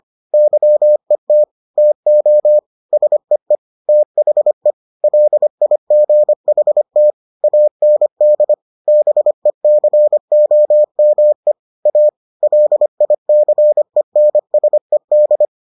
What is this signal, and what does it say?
Morse code, heard as a keyed tone, about 25 wpm, 600 Hz. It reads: YET TO SEE THE LIGHT AND BECOME A LICENSED